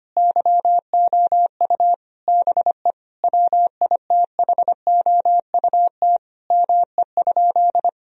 Morse code, heard as a keyed tone, about 25 wpm, 700 Hz. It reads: YOU BE WITHOUT ME?
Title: Sentences from Top 1000 Words - Review Part 3 - 25wpm